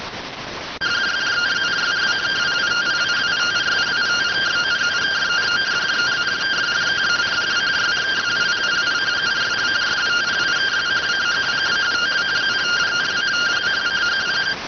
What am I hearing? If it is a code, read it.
Two-tone teleprinter signal, about 170 Hz shift.